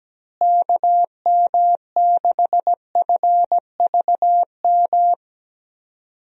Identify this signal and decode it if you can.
Morse code, heard as a keyed tone, about 17 wpm, 700 Hz. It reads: KM6FVM
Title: USA Call Signs - 6 Characters - Repeated 2x - 17wpm